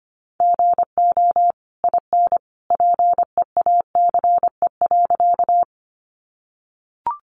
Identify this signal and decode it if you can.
Morse code, heard as a keyed tone, about 25 wpm, 700 Hz. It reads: GO IN PEACE.
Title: Sentences from Top 1500 Words 25wpm